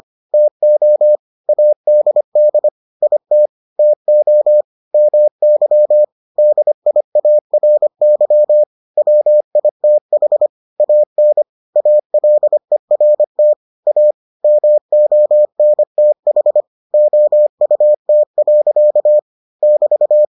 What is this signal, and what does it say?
Morse code, heard as a keyed tone, about 25 wpm, 600 Hz. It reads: TO ADD IT TO MY DIARY WITH AN ALERT A MONTH OUT. <BT>